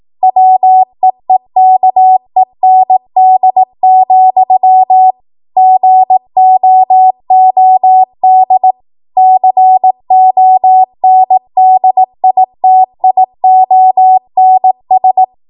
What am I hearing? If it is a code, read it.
Morse code, heard as a keyed tone, about 18 wpm, 750 Hz. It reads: WEEKEND, GOOD CONDITIONS
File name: Archive / 18 WPM / 18WPM